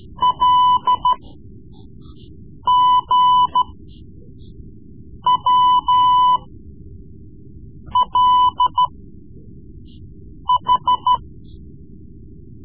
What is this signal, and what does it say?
Morse code, heard as a keyed tone, about 13 wpm, spaced as for 5 wpm, 1000 Hz. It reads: LGWLH
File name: Morse Lessons / Morse Lessons / Tape-3B